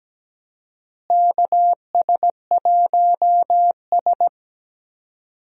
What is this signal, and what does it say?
Morse code, heard as a keyed tone, about 17 wpm, 700 Hz. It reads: KS1S
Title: USA Call Signs - 4 Characters - Repeated 2x - 17wpm